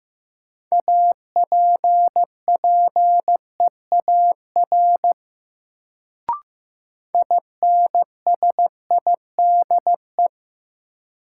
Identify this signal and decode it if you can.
Morse code, heard as a keyed tone, about 15 wpm, 700 Hz. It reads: APPEAR INSIDE
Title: Sentences from Top 1000 Words - Review Part 2 - New Word Drill - 15wpm